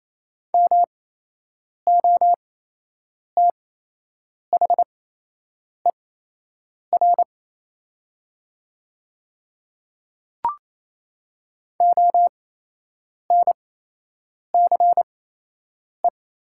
Morse code, heard as a keyed tone, about 28 wpm, spaced as for 8 wpm, 700 Hz. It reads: MOTHER ONCE
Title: 8x Character Spacing - Top 500 Words - 28wpm